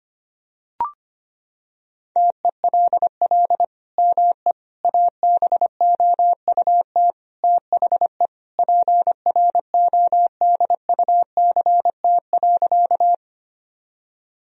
Morse code, heard as a keyed tone, about 25 wpm, 700 Hz. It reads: TELL ME ABOUT THE PRODUCT.
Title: Sentences from Top 1000 Words - Review Part 2 - 25wpm